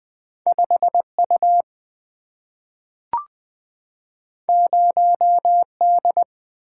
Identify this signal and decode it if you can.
Morse code, heard as a keyed tone, about 20 wpm, 700 Hz. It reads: 5 U 0 D
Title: Single-Letter Single-Number 20wpm